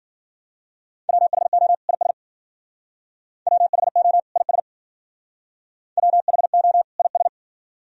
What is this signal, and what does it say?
Morse code, heard as a keyed tone, about 51 wpm, 700 Hz. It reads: WHO IS WHO IS WHO IS